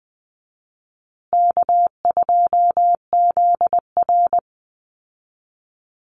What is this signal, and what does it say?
Morse code, heard as a keyed tone, about 20 wpm, 700 Hz. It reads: K2ZR
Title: Call Signs Repeated 2x - 20wpm